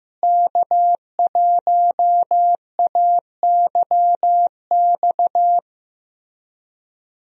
Morse code, heard as a keyed tone, about 15 wpm, 700 Hz. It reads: K1AYX
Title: USA Call Signs - 5 Characters - Repeated 3x - 15wpm